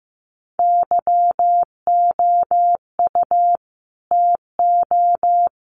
Morse code, heard as a keyed tone, about 15 wpm, 700 Hz. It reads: YOU TO
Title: Sets of 2 Words 15wpm